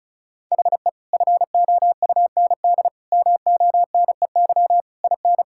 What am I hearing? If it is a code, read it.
Morse code, heard as a keyed tone, about 36 wpm, 700 Hz. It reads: HE FOUND MONEY IN
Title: Sentences from Top 1000 Words - Review Part 6 - 35wpm